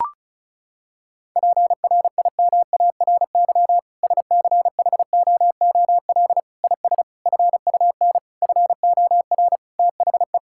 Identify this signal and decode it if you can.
Morse code, heard as a keyed tone, about 36 wpm, 700 Hz. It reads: PRIMARY SCHOOL IS FUN FOR THE
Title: Sentences from Top 1000 Words - Review Part 5 - 35wpm